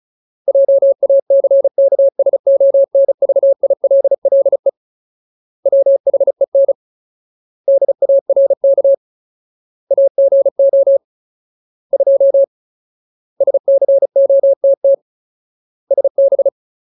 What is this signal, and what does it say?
Morse code, heard as a keyed tone, about 34 wpm, 550 Hz. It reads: JACKSONVILLE WHEN DARK AGO 2 SCOTT SB
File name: Ultimate Word List / 35WPM Ultimate